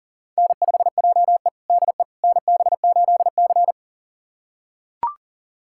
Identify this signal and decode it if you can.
Morse code, heard as a keyed tone, about 37 wpm, 700 Hz. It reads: N5JE DE NB8C
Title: USA Callsign DE Callsign - 4 Characters - 40wpm